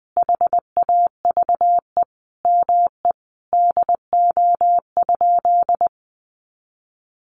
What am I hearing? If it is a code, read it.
Morse code, heard as a keyed tone, about 20 wpm, 700 Hz. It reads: HAVE ME DO?
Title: Sentences from Top 200 Words 20wpm